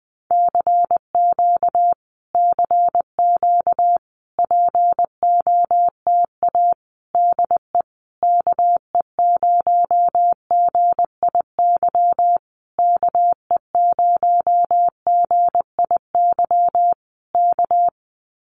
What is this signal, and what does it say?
Morse code, heard as a keyed tone, about 20 wpm, 700 Hz. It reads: CQ CQ POTA DE KE0GIY KE0GIY K
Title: Common QSO Phrases Encore 20wpm